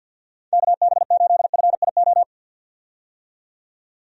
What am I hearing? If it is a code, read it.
Morse code, heard as a keyed tone, about 45 wpm, 700 Hz. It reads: KB8FIO
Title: USA Call Signs - 6 Characters - 50wpm